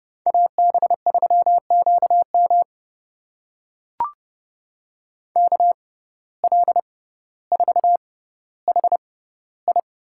Morse code, heard as a keyed tone, about 32 wpm, spaced as for 16 wpm, 700 Hz. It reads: AB3QM KL4HI
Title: USA Call Signs - 6x Character Spacing - 30wpm